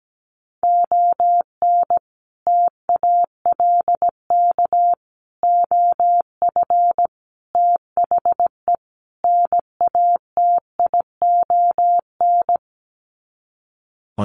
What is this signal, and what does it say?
Morse code, heard as a keyed tone, about 17 wpm, 700 Hz. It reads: ON TALK OF THE NATION
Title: Sets of 5 Words 17wpm